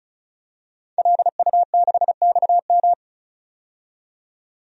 Morse code, heard as a keyed tone, about 36 wpm, 700 Hz. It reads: LU6XM
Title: DXCC-100 Call Signs - 5 Characters - 35wpm